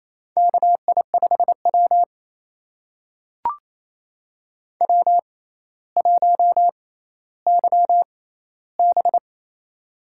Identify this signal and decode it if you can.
Morse code, heard as a keyed tone, about 28 wpm, spaced as for 14 wpm, 700 Hz. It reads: KI5W W1YB
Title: USA Call Signs - 6x Character Spacing - 28wpm